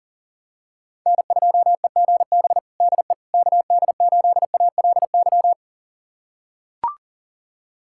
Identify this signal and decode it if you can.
Morse code, heard as a keyed tone, about 40 wpm, 700 Hz. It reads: N2EGB DE KD8ALY
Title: USA Callsign DE Callsign 40wpm